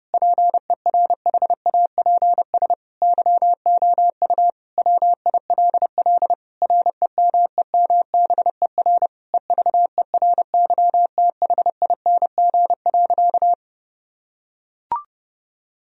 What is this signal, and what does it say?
Morse code, heard as a keyed tone, about 32 wpm, 700 Hz. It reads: PERHAPS YOU WILL REMEMBER EVERYTHING.
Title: Sentences from Top 1000 Words - Review Part 1 - 30wpm